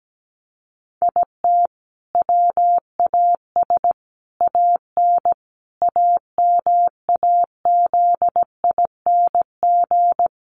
Morse code, heard as a keyed tone, about 17 wpm, 700 Hz. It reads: IT WAS AN AMAZING
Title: Sentences from Top 1500 Words 17wpm